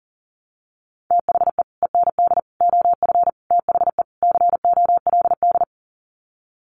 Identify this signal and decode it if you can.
Morse code, heard as a keyed tone, about 40 wpm, 700 Hz. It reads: THE END OF THE COLD
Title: Sets of 5 Words 40wpm